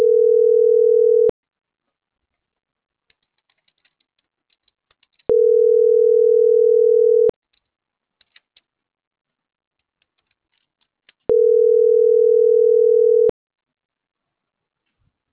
Ringback tone (North America), 440 + 480 Hz.